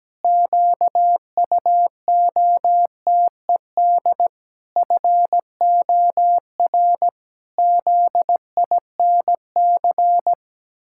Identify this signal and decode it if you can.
Morse code, heard as a keyed tone, about 17 wpm, 700 Hz. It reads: QUOTED FOR ZINC
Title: Pangram Challenge v2 - 17wpm